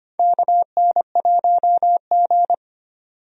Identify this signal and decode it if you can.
Morse code, heard as a keyed tone, about 25 wpm, 700 Hz. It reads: KN1G
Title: USA Call Signs - 4 Characters - 25wpm